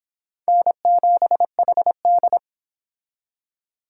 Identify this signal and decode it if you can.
Morse code, heard as a keyed tone, about 25 wpm, 700 Hz. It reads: N7HD